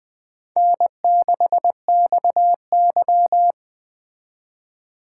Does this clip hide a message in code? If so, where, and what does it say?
Yes: Morse code from 0.6 to 3.5 s, heard as a keyed tone, about 21 wpm, 700 Hz. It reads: N6XY